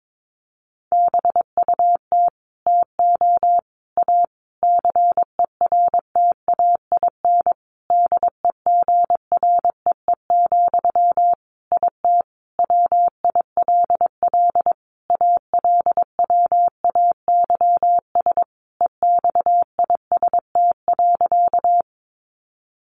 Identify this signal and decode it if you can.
Morse code, heard as a keyed tone, about 22 wpm, 700 Hz. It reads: BUT TO A CERTAIN DEGREE, IT WILL ALWAYS EXIST.
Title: Sentences from Top 1500 Words 22wpm